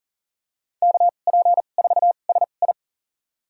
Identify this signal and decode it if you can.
Morse code, heard as a keyed tone, about 40 wpm, spaced as for 29 wpm, 700 Hz. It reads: KP4SI
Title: USA Call Signs - 5 Characters - 2x Character Spacing - 40wpm